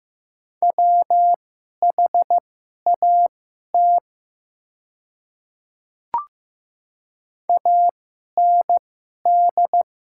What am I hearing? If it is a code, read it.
Morse code, heard as a keyed tone, about 15 wpm, spaced as for 11 wpm, 700 Hz. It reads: WHAT AND
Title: 2x Character Spacing - Top 100 Words - 15wpm